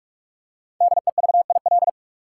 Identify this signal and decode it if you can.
Morse code, heard as a keyed tone, about 46 wpm, 700 Hz. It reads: DEVIL
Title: Five Letter Words - 45wpm